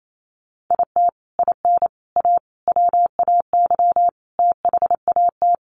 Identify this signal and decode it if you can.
Morse code, heard as a keyed tone, about 28 wpm, 700 Hz. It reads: IT IN A WAY THAT
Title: Sets of 5 Words Encore2 28wpm